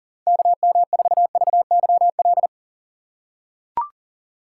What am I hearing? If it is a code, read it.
Morse code, heard as a keyed tone, about 39 wpm, 700 Hz. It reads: KM4VYL
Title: USA Call Signs - 2x Character Spacing - 40wpm